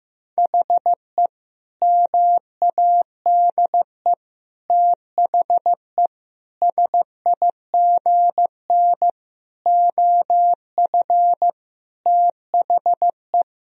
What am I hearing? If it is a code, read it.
Morse code, heard as a keyed tone, about 15 wpm, 700 Hz. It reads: HE MADE THE SIGN OF THE
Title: Sentences from Top 1000 Words - Review Part 1 - 15wpm